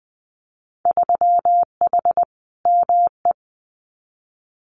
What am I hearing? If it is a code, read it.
Morse code, heard as a keyed tone, about 19 wpm, 700 Hz. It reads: 3H ME
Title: Winter Field Day Exchanges Encore 20wpm